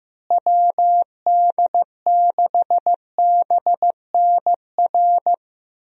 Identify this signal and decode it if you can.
Morse code, heard as a keyed tone, about 15 wpm, 700 Hz. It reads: WD6BNR
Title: USA Call Signs - 6 Characters - Repeated 3x - 15wpm